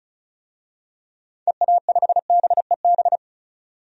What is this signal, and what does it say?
Morse code, heard as a keyed tone, about 34 wpm, 700 Hz. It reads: EA5BEB
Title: DXCC-100 Call Signs - 6 Characters - 35wpm